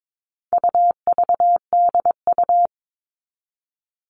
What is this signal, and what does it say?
Morse code, heard as a keyed tone, about 22 wpm, 700 Hz. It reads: UVDU
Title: All Cut Numbers - 4-Digit 22wpm